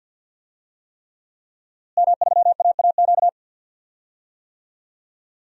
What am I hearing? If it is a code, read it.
Morse code, heard as a keyed tone, about 52 wpm, 700 Hz. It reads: M3AAQ
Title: DXCC-100 Call Signs - 5 Characters - Repeated 3x - 50wpm